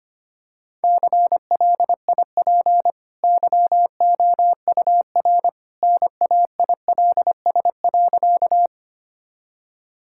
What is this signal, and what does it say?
Morse code, heard as a keyed tone, about 25 wpm, 700 Hz. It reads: CLIP YOUR NAILS.
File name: Easy Sentences with Hints - 25wpm